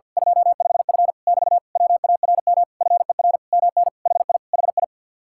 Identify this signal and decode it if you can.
Morse code, heard as a keyed tone, about 49 wpm, 700 Hz. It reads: 25F <BT> WARM FER MN HI HI
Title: Speed-Racing Common QSO Phrases - 50wpm